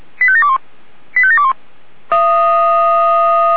Five-tone selective call (ZVEI), heard as quick runs of five tones, tones of about 70 ms. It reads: 85721 85721
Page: ZVEI